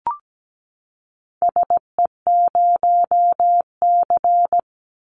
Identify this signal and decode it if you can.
Morse code, heard as a keyed tone, about 17 wpm, 700 Hz. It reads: SE0C
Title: Call Signs Repeated 3x - 17wpm